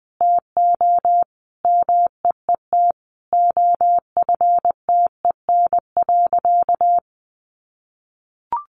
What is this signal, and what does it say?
Morse code, heard as a keyed tone, about 20 wpm, 700 Hz. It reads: TO MEET OFTEN.
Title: Sentences from Top 1100 Words 20wpm